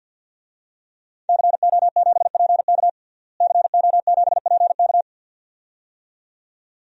Morse code, heard as a keyed tone, about 52 wpm, 700 Hz. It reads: CO7PK CO7PK
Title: DXCC-100 Call Signs - 5 Characters - Repeated 2x - 50wpm